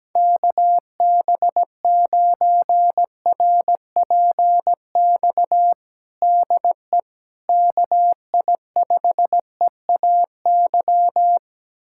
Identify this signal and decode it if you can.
Morse code, heard as a keyed tone, about 17 wpm, 700 Hz. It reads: KB9RPX DE KI5EAY
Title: USA Callsign DE Callsign - 6 Characters - 17wpm